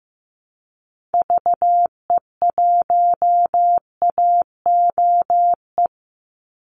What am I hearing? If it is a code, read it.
Morse code, heard as a keyed tone, about 15 wpm, 700 Hz. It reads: VE1AOE
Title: Call Signs Repeated 3x - 15wpm